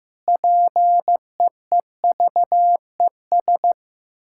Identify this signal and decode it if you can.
Morse code, heard as a keyed tone, about 15 wpm, 700 Hz. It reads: PEEVES
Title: Pangram Challenge v2 - 15wpm